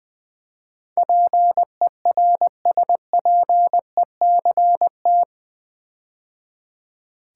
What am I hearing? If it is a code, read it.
Morse code, heard as a keyed tone, about 20 wpm, 700 Hz. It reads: PERSPECT
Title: Build A Word - 4 Letters At A Time - 11-Letter Words - 20wpm